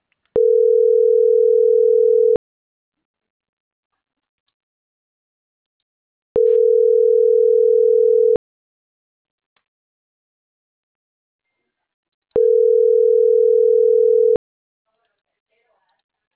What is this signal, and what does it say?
Ringback tone (North America), 440 + 480 Hz, with speech.
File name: CENTRAL NAVIERAS / out onelinecapacitacion - .27819